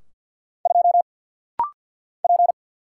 Morse code, heard as a keyed tone, about 50 wpm, 690 Hz. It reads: 2 P
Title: Single Letter-Number Alt - Rapid-Fire 50wpm